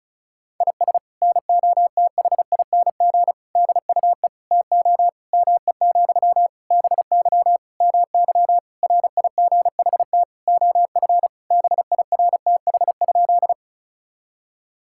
Morse code, heard as a keyed tone, about 36 wpm, 700 Hz. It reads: IS NOTHING DUE TO ME, BY MY RIGHT OF BIRTH?
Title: Sentences from Top 1000 Words - Review Part 6 - 35wpm